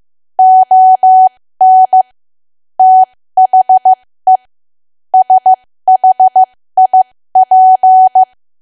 Morse code, heard as a keyed tone, about 15 wpm, spaced as for 13 wpm, 750 Hz. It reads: ON THE SHIP
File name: cwa / W1AW-Level2 / 13WPM